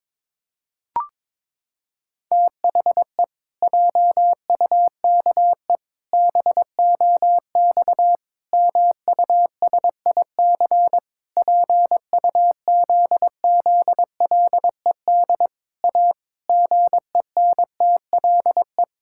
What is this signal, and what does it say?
Morse code, heard as a keyed tone, about 22 wpm, 700 Hz. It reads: THE JUKE BOX MUSIC PUZZLED A GENTLE
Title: Pangram Challenge v2 - 22wpm